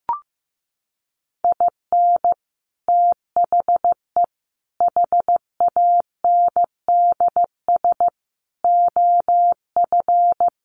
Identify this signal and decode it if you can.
Morse code, heard as a keyed tone, about 14 wpm, 700 Hz. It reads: IN THE HANDS OF
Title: Sets of 4 Words 15wpm